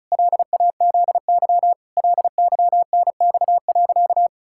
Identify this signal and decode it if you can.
Morse code, heard as a keyed tone, about 34 wpm, 700 Hz. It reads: LAZY LYNX.
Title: Pangram Challenge v2 - 35wpm